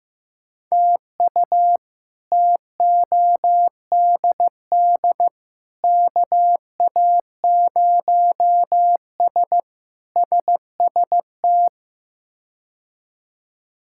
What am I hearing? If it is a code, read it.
Morse code, heard as a keyed tone, about 16 wpm, 700 Hz. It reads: TU TODD KA0S SST
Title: K1USN SST QSO Practice - 15wpm